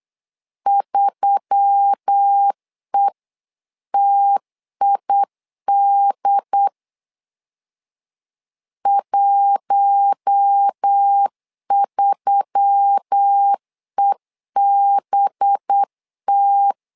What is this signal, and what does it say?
Morse code, heard as a keyed tone, about 8 wpm, 800 Hz. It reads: 3ETID 13EBT
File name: LEZIONI / 5 LBVC / 5 LBV 8wpm 2837